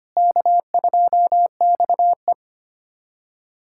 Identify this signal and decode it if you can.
Morse code, heard as a keyed tone, about 25 wpm, 700 Hz. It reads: K2XE
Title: USA Callsign DE Callsign - 4 Characters - 25wpm